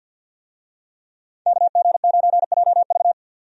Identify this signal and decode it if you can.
Morse code, heard as a keyed tone, about 50 wpm, 700 Hz. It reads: KG9JV
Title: USA Call Signs - 5 Characters - 50wpm